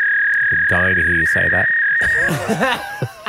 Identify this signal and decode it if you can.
Fax machine handshake, with speech.